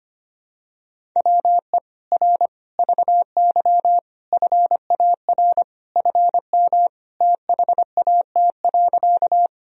Morse code, heard as a keyed tone, about 25 wpm, 700 Hz. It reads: WE R VY FAR FM THAT.
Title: Sentences from Top 300 Words and Abbreviations 25wpm